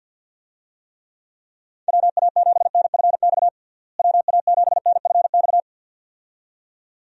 Morse code, heard as a keyed tone, about 52 wpm, 700 Hz. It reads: WA7NFX WA7NFX
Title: USA Call Signs - 6 Characters - Repeated 2x - 50wpm